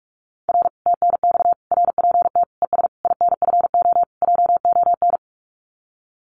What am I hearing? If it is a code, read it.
Morse code, heard as a keyed tone, about 46 wpm, 700 Hz. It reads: R TNX RPT ES INFO JON
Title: Common QSO Phrases Encore 45wpm